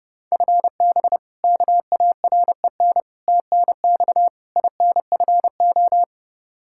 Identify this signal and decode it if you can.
Morse code, heard as a keyed tone, about 30 wpm, 700 Hz. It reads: FB KAREN TNX INFO
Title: K1USN SST QSO Practice 30wpm